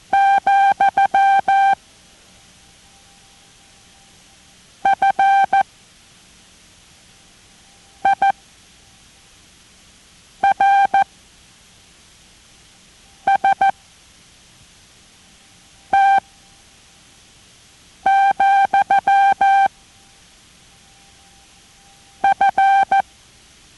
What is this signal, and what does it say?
Morse code, heard as a keyed tone, about 14 wpm, spaced as for 4 wpm, 790 Hz. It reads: , FIRST, F